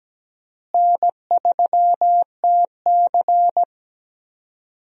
Morse code, heard as a keyed tone, about 17 wpm, 700 Hz. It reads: N3TC
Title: USA Call Signs - 4 Characters - Repeated 3x - 17wpm